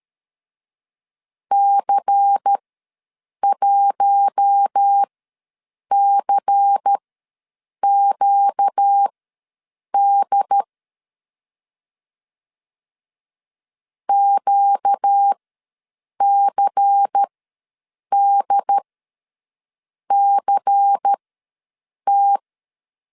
Morse code, heard as a keyed tone, about 13 wpm, spaced as for 7 wpm, 800 Hz. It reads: C1CQD QCDCT
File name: LEZIONI / 5 LBVC / 5 LBVC ws 12wpm 3159